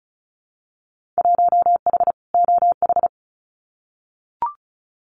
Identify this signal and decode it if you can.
Morse code, heard as a keyed tone, about 34 wpm, 700 Hz. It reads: 1H OH
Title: Winter Field Day Exchanges 35wpm